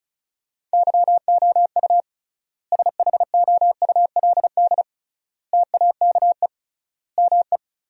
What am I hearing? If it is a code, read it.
Morse code, heard as a keyed tone, about 36 wpm, 700 Hz. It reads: YOU SHOULD TAKE ME
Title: 3x Word-Spacing - Sentences from Top 300 Words - 35wpm